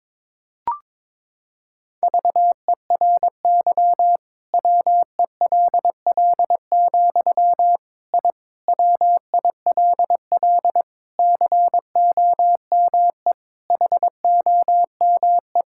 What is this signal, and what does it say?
Morse code, heard as a keyed tone, about 22 wpm, 700 Hz. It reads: VERY WELL, I WILL COME HOME
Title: Sentences from Top 1000 Words - Review Part 5 - 22wpm